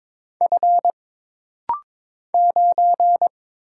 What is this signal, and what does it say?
Morse code, heard as a keyed tone, about 22 wpm, 700 Hz. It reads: F 9
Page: This rapid-fire timing is geared to push your ICR (Instant Character Recognition) skills.